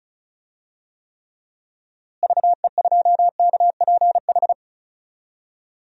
Morse code, heard as a keyed tone, about 34 wpm, 700 Hz. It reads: VE2KPH
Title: DXCC-100 Call Signs - 6 Characters - Repeated 2x - 35wpm